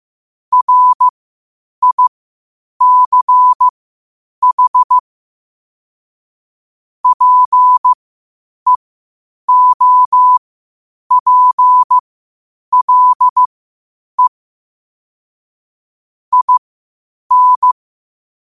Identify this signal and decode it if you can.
Morse code, heard as a keyed tone, about 15 wpm, spaced as for 8 wpm, 1000 Hz. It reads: RICH PEOPLE IN
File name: qotdfiles / 8wpm